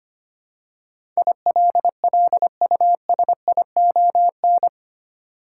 Morse code, heard as a keyed tone, about 25 wpm, 700 Hz. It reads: ILLUSION
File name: 8 Letter Words - First 3 Letters Given - 25wpm